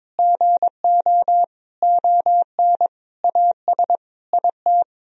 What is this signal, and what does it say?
Morse code, heard as a keyed tone, about 22 wpm, 700 Hz. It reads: GO ON AS IT
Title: Sentences from Top 1000 Words - Review Part 3 - 22wpm